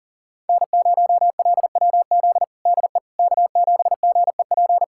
Morse code, heard as a keyed tone, about 40 wpm, 700 Hz. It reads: N0LWZ DE K7GEP
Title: USA Callsign DE Callsign - 5 Characters - 40wpm